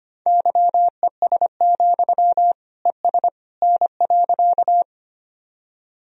Morse code, heard as a keyed tone, about 25 wpm, 700 Hz. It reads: YES, ES N.
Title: Sentences from Top 500 Words and Abbreviations 25wpm